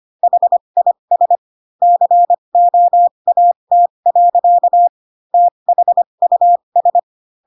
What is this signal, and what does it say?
Morse code, heard as a keyed tone, about 25 wpm, spaced as for 22 wpm, 700 Hz. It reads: HIS COAT. THUS
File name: Pinocchio / Pinocchio ch2 18 25 ik1hge